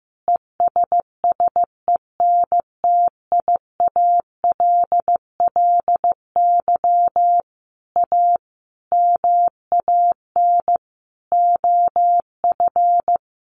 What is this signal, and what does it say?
Morse code, heard as a keyed tone, about 15 wpm, 700 Hz. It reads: ESSENTIALLY A MAN OF
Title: Sentences from Top 1300 Words 15wpm